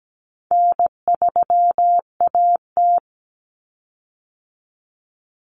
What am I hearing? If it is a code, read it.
Morse code, heard as a keyed tone, about 17 wpm, 700 Hz. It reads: N3AT
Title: Common Cut Numbers - 4-Digit 17wpm